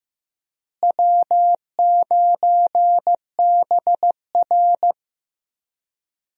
Morse code, heard as a keyed tone, about 15 wpm, 700 Hz. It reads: W9BR
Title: USA Call Signs - 4 Characters - Repeated 3x - 15wpm